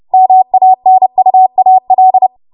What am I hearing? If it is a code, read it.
Morse code, heard as a keyed tone, about 30 wpm, 750 Hz. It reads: MANUAL
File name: Morse / 30WPM